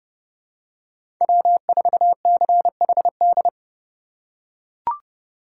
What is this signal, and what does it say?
Morse code, heard as a keyed tone, about 31 wpm, 700 Hz. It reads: W4CHD
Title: USA Call Signs - 5 Characters - 2x Character Spacing - 30wpm